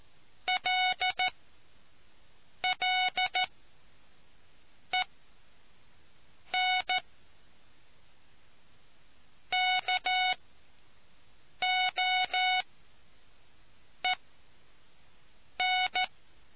Morse code, heard as a keyed tone, about 14 wpm, spaced as for 5 wpm, 740 Hz. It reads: LLENKOEN